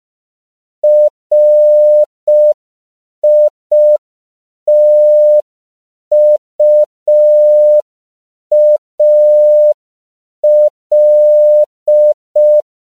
Morse code, heard as a keyed tone, about 5 wpm, 600 Hz. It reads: RITUAL
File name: CW / 5PPM